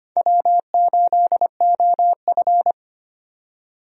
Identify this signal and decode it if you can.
Morse code, heard as a keyed tone, about 25 wpm, 700 Hz. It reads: W8OF
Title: USA Call Signs - 4 Characters - 25wpm